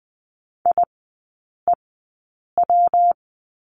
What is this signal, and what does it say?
Morse code, heard as a keyed tone, about 20 wpm, spaced as for 8 wpm, 700 Hz. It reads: IEW
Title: 178 Lesson 29e - Slash - Three Characters 20wpm